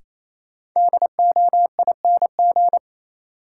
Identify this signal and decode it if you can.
Morse code, heard as a keyed tone, about 28 wpm, 700 Hz. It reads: DOING
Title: Sets of 4 Words - New Word Drill - 28wpm